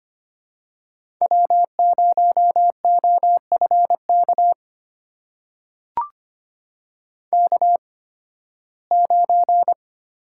Morse code, heard as a keyed tone, about 25 wpm, spaced as for 16 wpm, 700 Hz. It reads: W0OFK K9
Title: USA Call Signs - 5 Characters - 8x Character Spacing - 25wpm